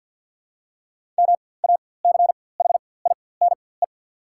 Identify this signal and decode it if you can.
Morse code, heard as a keyed tone, about 51 wpm, spaced as for 23 wpm, 690 Hz. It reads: MACHINE
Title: 4x Character Spacing - Top 500 Words - 50wpm